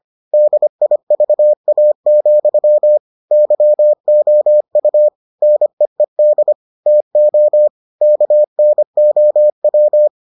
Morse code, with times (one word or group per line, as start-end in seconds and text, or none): 0.3-3.0 DIVA,
3.3-5.1 YOU
5.4-6.5 NEED
6.9-7.7 TO
8.0-10.1 KNOW